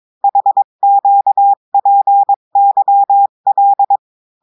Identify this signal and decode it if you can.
Morse code, heard as a keyed tone, about 22 wpm, 800 Hz. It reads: HQPYL
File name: cours-du-mercredi / audios / cours-853-du-2-mai-2024-etxte chiffre et texte clair-a-20-wpm